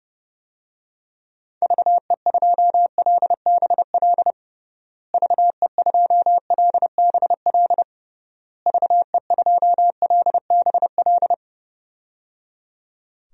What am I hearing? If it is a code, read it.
Morse code, heard as a keyed tone, about 30 wpm, 700 Hz. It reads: VE2LBL VE2LBL VE2LBL
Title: DXCC-100 Call Signs - 6 Characters - Repeated 3x - 30wpm